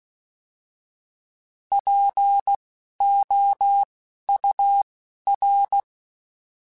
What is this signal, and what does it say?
Morse code, heard as a keyed tone, about 16 wpm, spaced as for 11 wpm, 800 Hz. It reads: POUR
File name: Alphabetical / Alphabetical mp3s / Ps @10